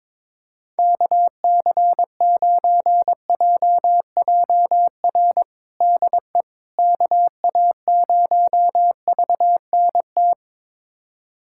Morse code, heard as a keyed tone, about 21 wpm, 700 Hz. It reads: KC9JJR DE KA0VNT
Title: USA Callsign DE Callsign - 6 Characters - 22wpm